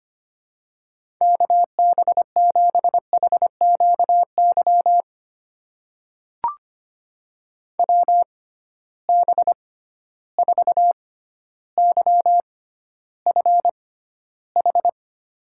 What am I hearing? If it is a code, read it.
Morse code, heard as a keyed tone, about 25 wpm, spaced as for 13 wpm, 700 Hz. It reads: KB7HQY WB4YFH
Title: USA Call Signs - 6 Characters - 6x Character Spacing - 25wpm